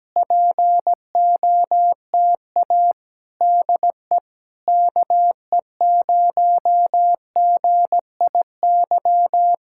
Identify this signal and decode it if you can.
Morse code, heard as a keyed tone, about 17 wpm, 700 Hz. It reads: POTA DE KE0GIY
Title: Speed-Racing Common QSO Phrases - 17wpm